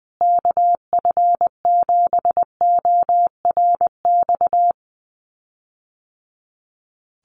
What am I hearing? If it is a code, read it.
Morse code, heard as a keyed tone, about 20 wpm, 700 Hz. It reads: KF7ORX
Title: USA Call Signs Repeated 3x 20wpm